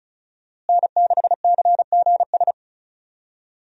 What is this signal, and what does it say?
Morse code, heard as a keyed tone, about 36 wpm, 700 Hz. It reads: N6CGS
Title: USA Call Signs - 5 Characters - Repeated 3x - 35wpm